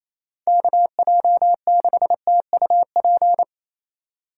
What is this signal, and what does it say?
Morse code, heard as a keyed tone, about 37 wpm, 700 Hz. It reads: KJ6TUP